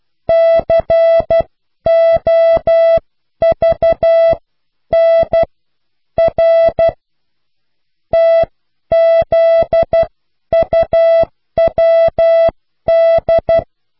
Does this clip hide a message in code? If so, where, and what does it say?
Yes: Morse code from 0.3 to 13.6 s, heard as a keyed tone, about 12 wpm, spaced as for 10 wpm, 660 Hz. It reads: COVNR TZUWD